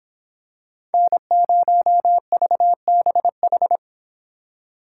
Morse code, heard as a keyed tone, about 25 wpm, 700 Hz. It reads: N0VBH